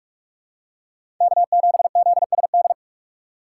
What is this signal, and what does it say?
Morse code, heard as a keyed tone, about 43 wpm, 700 Hz. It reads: K7ZSD
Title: USA Call Signs - 5 Characters - 4x Character Spacing - 45wpm